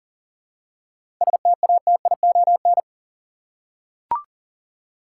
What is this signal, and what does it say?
Morse code, heard as a keyed tone, about 40 wpm, 700 Hz. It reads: STATION
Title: Seven Letter Words - 40wpm